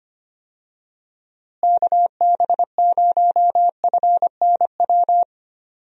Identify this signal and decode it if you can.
Morse code, heard as a keyed tone, about 25 wpm, 700 Hz. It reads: KB0FNW
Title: USA Call Signs - 6 Characters - Repeated 2x - 25wpm